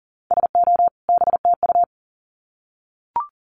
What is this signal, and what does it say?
Morse code, heard as a keyed tone, about 40 wpm, 700 Hz. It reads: SO BTU
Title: Common QSO Phrases 40wpm